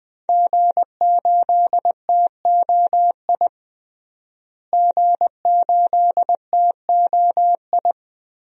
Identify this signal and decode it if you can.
Morse code, heard as a keyed tone, about 20 wpm, 700 Hz. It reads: G8TOI G8TOI
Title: DXCC-100 Call Signs - 5 Characters - Repeated 3x - 20wpm